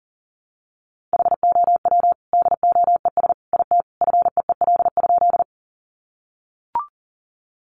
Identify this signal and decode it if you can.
Morse code, heard as a keyed tone, about 40 wpm, 700 Hz. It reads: HOW DOES IT FEEL?
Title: Sentences from Top 400 Words 40wpm